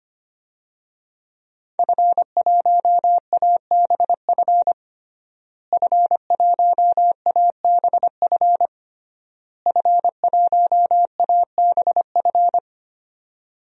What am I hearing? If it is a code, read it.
Morse code, heard as a keyed tone, about 25 wpm, 700 Hz. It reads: F1ABF F1ABF F1ABF
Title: DXCC-100 Call Signs - 5 Characters - Repeated 3x - 25wpm